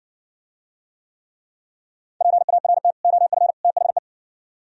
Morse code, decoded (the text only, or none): PART OF THE